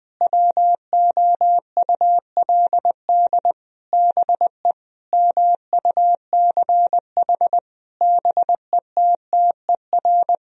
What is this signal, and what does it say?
Morse code, heard as a keyed tone, about 20 wpm, 700 Hz. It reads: WOULD BE MUCH BETTER
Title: Sentences from Top 1000 Words - Review Part 3 - 20wpm